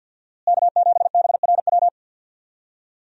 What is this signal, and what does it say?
Morse code, heard as a keyed tone, about 52 wpm, 700 Hz. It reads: K7BRW
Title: USA Call Signs - 5 Characters - Repeated 3x - 50wpm